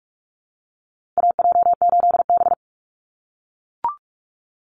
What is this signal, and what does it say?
Morse code, heard as a keyed tone, about 44 wpm, 700 Hz. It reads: AJ8B
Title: Call Signs - 45wpm